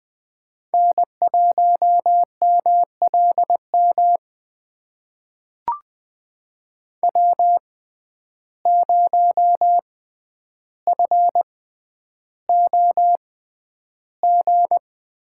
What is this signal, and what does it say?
Morse code, heard as a keyed tone, about 20 wpm, spaced as for 10 wpm, 700 Hz. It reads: N1MLM W0FOG
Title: USA Call Signs - 5 Characters - 6x Character Spacing - 20wpm